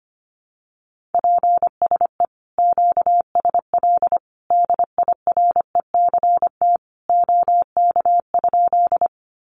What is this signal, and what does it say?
Morse code, heard as a keyed tone, about 25 wpm, 700 Hz. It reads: PSE QSL DIRECT OK?
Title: Common QSO Phrases 25wpm